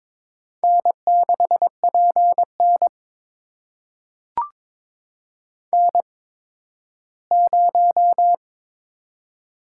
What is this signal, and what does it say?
Morse code, heard as a keyed tone, about 21 wpm, spaced as for 13 wpm, 700 Hz. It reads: N6PN N0
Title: USA Call Signs - 4 Characters - 8x Character Spacing - 22wpm